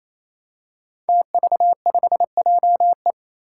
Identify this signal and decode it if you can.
Morse code, heard as a keyed tone, about 28 wpm, 700 Hz. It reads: TV5JE
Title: Five Challenging Letter-Number Combinations 28wpm